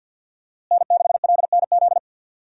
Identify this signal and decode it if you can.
Morse code, heard as a keyed tone, about 52 wpm, 700 Hz. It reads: N6LNZ
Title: USA Call Signs - 5 Characters - Repeated 2x - 50wpm